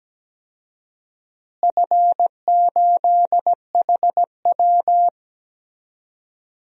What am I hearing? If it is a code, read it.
Morse code, heard as a keyed tone, about 17 wpm, 700 Hz. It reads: F8HW
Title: DXCC-100 Call Signs - 4 Characters - Repeated 3x - 17wpm